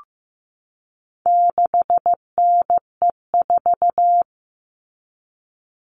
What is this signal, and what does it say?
Morse code, heard as a keyed tone, about 15 wpm, 700 Hz. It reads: 6NE4
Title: All Cut Numbers - 4-Digit 15wpm